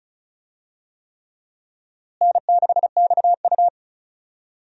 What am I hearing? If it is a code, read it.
Morse code, heard as a keyed tone, about 36 wpm, 700 Hz. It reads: N6XU
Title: USA Call Signs - 4 Characters - Repeated 3x - 35wpm